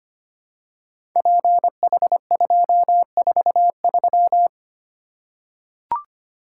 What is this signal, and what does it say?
Morse code, heard as a keyed tone, about 25 wpm, 700 Hz. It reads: PH243
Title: Code Groups - 5 Letter:Numbers - 25wpm